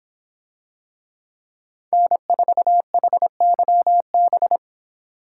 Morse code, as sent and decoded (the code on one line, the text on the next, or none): -. ....- .... -.-- -...
N4HYB